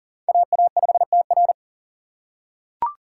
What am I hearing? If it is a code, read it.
Morse code, heard as a keyed tone, about 37 wpm, 700 Hz. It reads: AA5TR
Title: USA Call Signs - 5 Characters - 2x Character Spacing - 40wpm